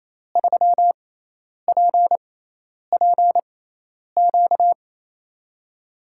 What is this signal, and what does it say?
Morse code, heard as a keyed tone, about 28 wpm, spaced as for 10 wpm, 700 Hz. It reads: 3PPQ
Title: USA Call Signs - 6x Character Spacing - 28wpm